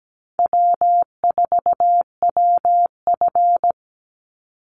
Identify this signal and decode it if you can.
Morse code, heard as a keyed tone, about 17 wpm, 700 Hz. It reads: W4WF
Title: Call Signs - 17wpm